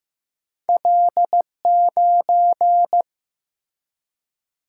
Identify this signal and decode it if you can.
Morse code, heard as a keyed tone, about 15 wpm, 700 Hz. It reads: L9
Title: Single-Letter Single-Number 15wpm